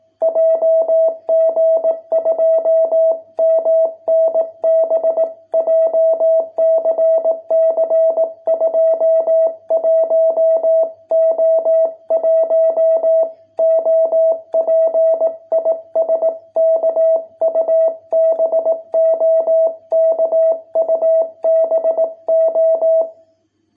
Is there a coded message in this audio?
Yes: Morse code from 0.2 to 23.0 s, heard as a keyed tone, about 18 wpm, 650 Hz. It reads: JG2MNB JCC 21O1 OP IS KUBO KUBO